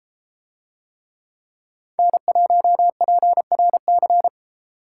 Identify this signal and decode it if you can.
Morse code, heard as a keyed tone, about 32 wpm, 700 Hz. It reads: N1PRC